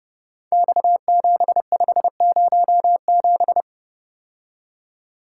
Morse code, heard as a keyed tone, about 30 wpm, 700 Hz. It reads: X7507
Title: Code Groups - 5 Letter:Numbers - 30wpm